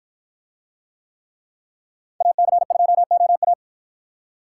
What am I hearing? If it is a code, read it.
Morse code, heard as a keyed tone, about 51 wpm, 700 Hz. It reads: AC2OA